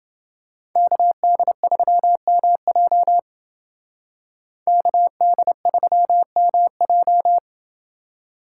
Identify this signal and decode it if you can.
Morse code, heard as a keyed tone, about 29 wpm, 700 Hz. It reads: KD3MJ KD3MJ